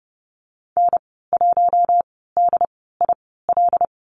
Morse code, heard as a keyed tone, about 30 wpm, spaced as for 17 wpm, 700 Hz. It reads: N1DIL
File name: Lesson 14g - Number 2 - Call Signs 30wpm